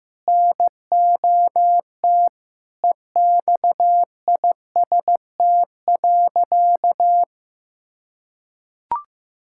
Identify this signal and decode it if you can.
Morse code, heard as a keyed tone, about 15 wpm, 700 Hz. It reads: NOT EXIST.
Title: Sentences from Top 1000 Words - Review Part 6 - 15wpm